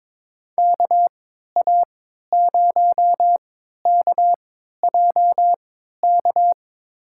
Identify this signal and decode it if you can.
Morse code, heard as a keyed tone, about 23 wpm, spaced as for 13 wpm, 700 Hz. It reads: KA0KJK
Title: USA Call Signs - 6 Characters - 3x Character Spacing - 22wpm